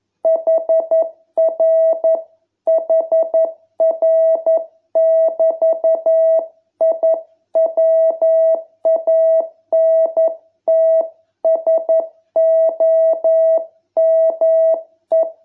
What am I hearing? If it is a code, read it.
Morse code, heard as a keyed tone, about 11 wpm, 660 Hz. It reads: HR HR<BT>I WANT SOME